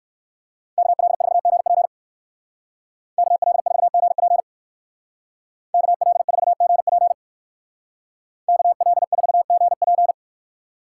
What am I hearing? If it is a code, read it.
Morse code, heard as a keyed tone, about 59 wpm, 700 Hz. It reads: KL4GP KL4GP KL4GP KL4GP